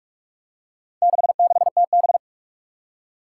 Morse code, heard as a keyed tone, about 47 wpm, 700 Hz. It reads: 66TB
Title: Four Challenging Letter-Number Combinations 45wpm